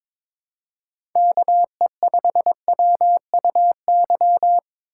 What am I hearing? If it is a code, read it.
Morse code, heard as a keyed tone, about 26 wpm, 700 Hz. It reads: KE5WUY
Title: Speed Racing - USA 2x3 Call Signs 22wpm